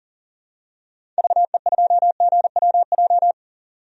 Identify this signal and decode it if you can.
Morse code, heard as a keyed tone, about 40 wpm, 700 Hz. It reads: VE2GWJ
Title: DXCC-100 Call Signs - 6 Characters - 40wpm